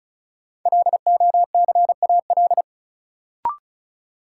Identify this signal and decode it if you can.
Morse code, heard as a keyed tone, about 36 wpm, 700 Hz. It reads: LOCAL
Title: Sentences from Top 1500 Words - New Word Drill Alt - 35wpm